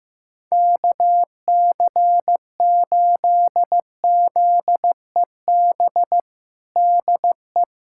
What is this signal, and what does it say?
Morse code, heard as a keyed tone, about 15 wpm, 700 Hz. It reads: KC8ZEB DE
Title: USA Callsign DE Callsign 15wpm